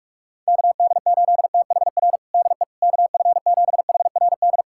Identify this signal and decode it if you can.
Morse code, heard as a keyed tone, about 46 wpm, 700 Hz. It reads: KD8THR DE KF7HRD
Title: USA Callsign DE Callsign - 6 Characters - 45wpm